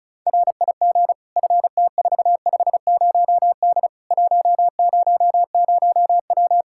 Morse code, heard as a keyed tone, about 34 wpm, 700 Hz. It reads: RIG FT450D 100W
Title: K1USN SST QSO Practice 35wpm